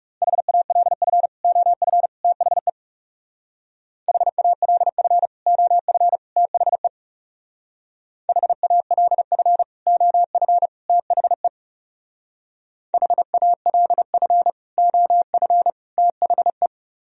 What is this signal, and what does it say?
Morse code, heard as a keyed tone, about 39 wpm, 700 Hz. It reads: HALF OF THE HALF OF THE HALF OF THE HALF OF THE